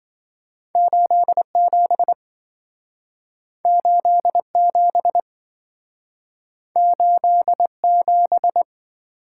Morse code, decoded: 87 87 87

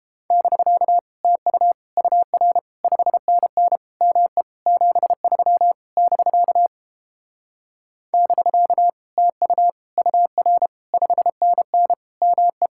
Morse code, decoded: <BK> TU UR 5NN ME 73 <BK> <BK> TU UR 5NN ME

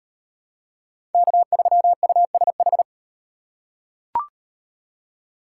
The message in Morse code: K3USH

